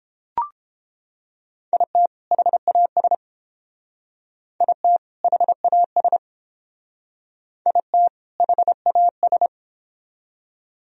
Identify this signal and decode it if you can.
Morse code, heard as a keyed tone, about 29 wpm, 700 Hz. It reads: IT HAS IT HAS IT HAS